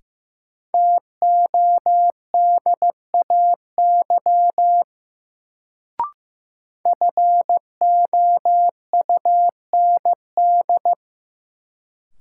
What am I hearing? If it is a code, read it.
Morse code, heard as a keyed tone, about 15 wpm, 700 Hz. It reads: TODAY FOUND
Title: Sets of 5 Words Encore - New Word Drill - 15wpm